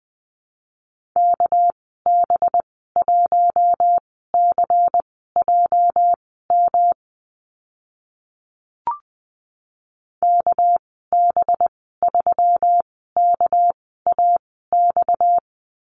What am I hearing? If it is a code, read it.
Morse code, heard as a keyed tone, about 20 wpm, spaced as for 14 wpm, 700 Hz. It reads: KB1CJM KB3KAX
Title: 245 Lesson 39g - Letter z - Call Signs 20wpm